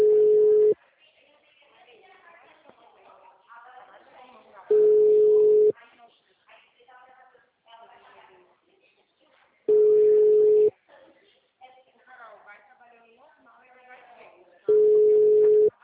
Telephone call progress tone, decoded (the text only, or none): ringback tone